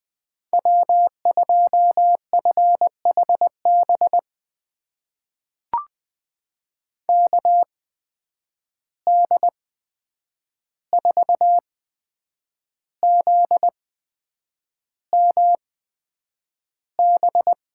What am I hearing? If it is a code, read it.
Morse code, heard as a keyed tone, about 20 wpm, spaced as for 8 wpm, 700 Hz. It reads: W2FHB KD4ZMB
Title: USA Call Signs - 8x Character Spacing - 20wpm